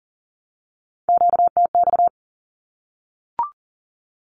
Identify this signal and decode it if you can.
Morse code, heard as a keyed tone, about 40 wpm, 700 Hz. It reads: QTX
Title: Q-Codes 40wpm